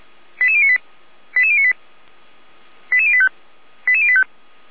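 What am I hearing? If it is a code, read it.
Five-tone selective call (ZVEI), heard as quick runs of five tones, tones of about 70 ms. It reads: 88098 88098 88085 88085
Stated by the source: ZVEI five-tone selective call received on an SDR.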